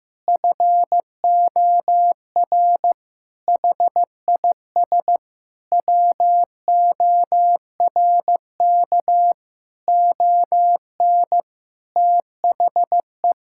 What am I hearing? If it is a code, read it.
Morse code, heard as a keyed tone, about 15 wpm, 700 Hz. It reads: FOR HIS WORK ON THE
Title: Sentences from Top 1000 Words - Review Part 5 - 15wpm